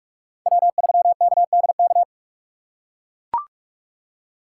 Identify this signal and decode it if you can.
Morse code, heard as a keyed tone, about 44 wpm, 690 Hz. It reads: W3KDK
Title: USA Call Signs - 5 Characters - 6x Character Spacing - 45wpm